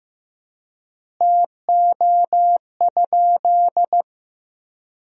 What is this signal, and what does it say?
Morse code, heard as a keyed tone, about 15 wpm, 700 Hz. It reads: TO?
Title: 4x Word-Spacing - Sentences from Top 200 Words - 15wpm